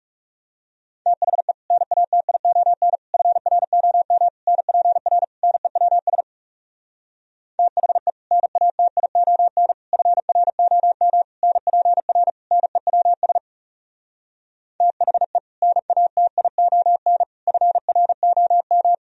Morse code, heard as a keyed tone, about 39 wpm, 700 Hz. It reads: THE NATION FROM NPR NEWS THE NATION FROM NPR NEWS THE NATION FROM